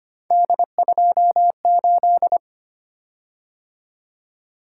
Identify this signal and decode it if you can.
Morse code, heard as a keyed tone, about 25 wpm, 700 Hz. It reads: D28
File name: Three Letter-Number Combinations - 25wpm